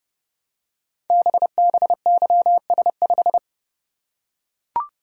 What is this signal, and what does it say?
Morse code, heard as a keyed tone, about 30 wpm, 700 Hz. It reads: BBYS5
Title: Code Groups - 5 Letter:Numbers - 30wpm